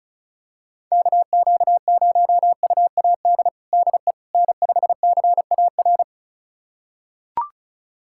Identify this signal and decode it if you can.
Morse code, heard as a keyed tone, about 36 wpm, 700 Hz. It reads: KQ0UAD DE N5CAR
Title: USA Callsign DE Callsign 35wpm